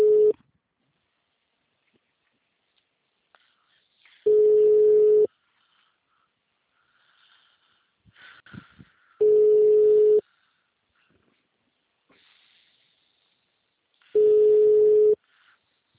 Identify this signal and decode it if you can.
Ringback tone (Europe, 425 Hz).
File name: out- -2007- - - .245